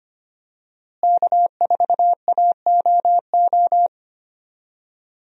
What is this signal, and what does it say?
Morse code, heard as a keyed tone, about 25 wpm, 700 Hz. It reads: K4AOO
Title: USA Call Signs - 5 Characters - Repeated 3x - 25wpm